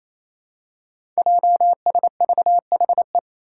Morse code, heard as a keyed tone, about 28 wpm, 700 Hz. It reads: JSVHE
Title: Five Challenging Letter-Number Combinations 28wpm